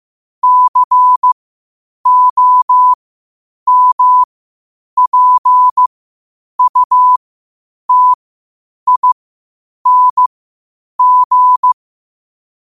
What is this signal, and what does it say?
Morse code, heard as a keyed tone, about 15 wpm, spaced as for 8 wpm, 1000 Hz. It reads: COMPUTING
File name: qotdfiles / 8wpm